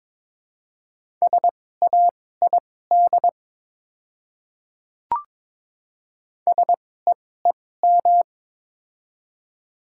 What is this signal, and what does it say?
Morse code, heard as a keyed tone, about 22 wpm, spaced as for 16 wpm, 700 Hz. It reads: SAID SEEM
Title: 2x Character Spacing - Top 500 Words - 22wpm